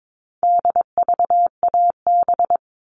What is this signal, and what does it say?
Morse code, heard as a keyed tone, about 22 wpm, 700 Hz. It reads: DVAB
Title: All Cut Numbers - 4-Digit 22wpm